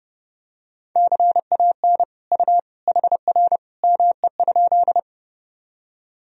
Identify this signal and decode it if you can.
Morse code, heard as a keyed tone, about 30 wpm, 700 Hz. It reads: CAN U HR ME?
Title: Sentences from Top 300 Words and Abbreviations 30wpm